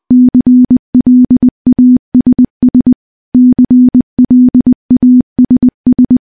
Morse code, heard as a keyed tone, about 20 wpm, 250 Hz. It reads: CLASS CLASS
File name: morse-training-20WPM